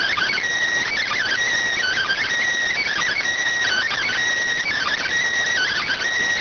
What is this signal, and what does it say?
Emergency Alert System (SAME) tones (header bursts), heard as harsh buzzing bursts. It is heard from 0.2 to 5.4 s.